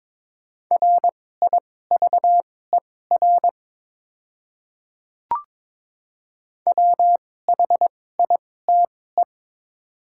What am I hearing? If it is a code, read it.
Morse code, heard as a keyed tone, about 22 wpm, spaced as for 16 wpm, 700 Hz. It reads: RIVER WHITE
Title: 2x Character Spacing - Top 300 Words - 22wpm